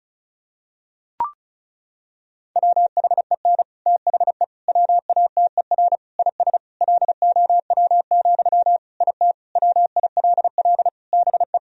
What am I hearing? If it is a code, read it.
Morse code, heard as a keyed tone, about 36 wpm, 700 Hz. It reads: WHEN THE WATER IS LOW, IT WILL BE
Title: Sentences from Top 1000 Words - Review Part 1 - 35wpm